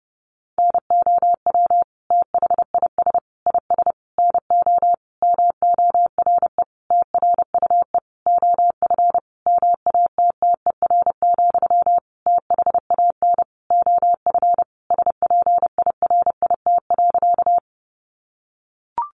Morse code, heard as a keyed tone, about 28 wpm, 700 Hz. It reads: NOW THIS IS NO MORE TRUE OF MATTER, THAN OF SPIRIT.
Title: Sentences from Top 1400 Words 30wpm